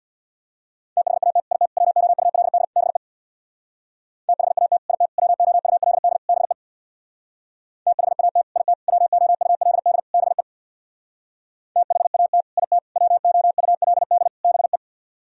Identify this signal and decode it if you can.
Morse code, heard as a keyed tone, about 64 wpm, 700 Hz. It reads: THAT IT WOULD BE THAT IT WOULD BE THAT IT WOULD BE THAT IT WOULD BE